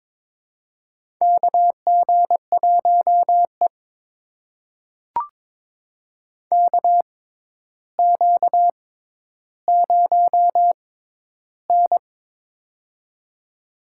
Morse code, heard as a keyed tone, about 22 wpm, spaced as for 11 wpm, 700 Hz. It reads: KG1E KQ0N
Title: USA Call Signs - 4 Characters - 6x Character Spacing - 22wpm